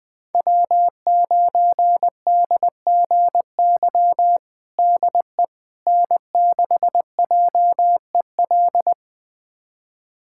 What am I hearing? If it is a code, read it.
Morse code, heard as a keyed tone, about 20 wpm, 700 Hz. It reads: W9DGY DE N6JEL
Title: USA Callsign DE Callsign - 5 Characters - 20wpm